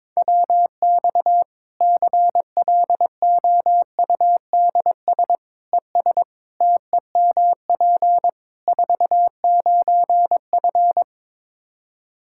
Morse code, heard as a keyed tone, about 22 wpm, 700 Hz. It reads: WX CLOUDS ES TEMP 49F
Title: K1USN SST QSO Practice 22wpm